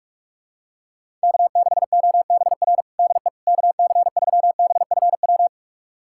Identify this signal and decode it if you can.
Morse code, heard as a keyed tone, about 45 wpm, 700 Hz. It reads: K6OBR DE KC3BFW
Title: USA Callsign DE Callsign 45wpm